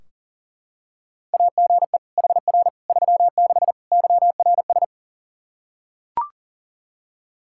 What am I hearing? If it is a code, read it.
Morse code, heard as a keyed tone, about 37 wpm, 700 Hz. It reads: AGE HR 36 YRS
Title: Practice QSO - Exchange 3 - More Info - 40wpm